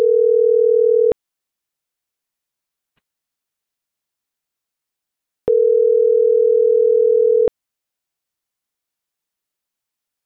Ringback tone (North America), 440 + 480 Hz.